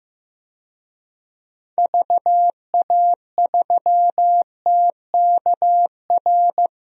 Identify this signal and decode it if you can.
Morse code, heard as a keyed tone, about 15 wpm, 700 Hz. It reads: VA3TKR
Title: DXCC-100 Call Signs - 6 Characters - Repeated 2x - 15wpm